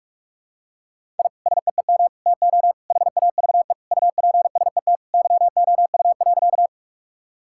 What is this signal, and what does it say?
Morse code, heard as a keyed tone, about 46 wpm, 700 Hz. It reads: I SEEM TO HAVE UPSET YOU.
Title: Sentences from Top 1000 Words - Review Part 5 - 45wpm